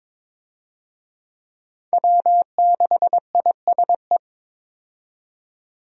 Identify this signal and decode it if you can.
Morse code, heard as a keyed tone, about 22 wpm, 700 Hz. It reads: W6ISE
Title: USA Call Signs - 5 Characters - 22wpm